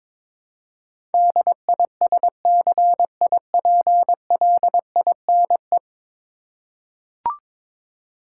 Morse code, heard as a keyed tone, about 22 wpm, 700 Hz. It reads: DISCIPLINE
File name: practice / 22wpm / 10 Letter Words - First 3 Letters Given - 22wpm